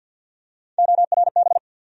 Morse code, heard as a keyed tone, about 53 wpm, 700 Hz. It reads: ORB
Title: USA Call Sign Postfixes - 3 Letters 50wpm